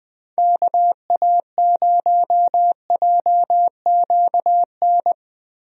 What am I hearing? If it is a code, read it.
Morse code, heard as a keyed tone, about 20 wpm, 700 Hz. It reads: KA0JQN
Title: USA Call Signs - 6 Characters - Repeated 3x - 20wpm